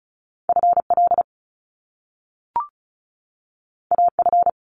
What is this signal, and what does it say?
Morse code, heard as a keyed tone, about 36 wpm, 700 Hz. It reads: FL AF
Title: Top Two Letter Combinations in Words 35wpm